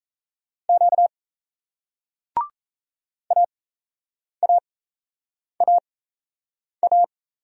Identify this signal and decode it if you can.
Morse code, heard as a keyed tone, about 36 wpm, spaced as for 8 wpm, 700 Hz. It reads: Q AAAA